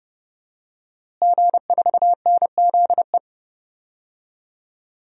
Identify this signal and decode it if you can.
Morse code, heard as a keyed tone, about 30 wpm, 700 Hz. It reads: G4NZE
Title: DXCC-100 Call Signs - 5 Characters - Repeated 2x - 30wpm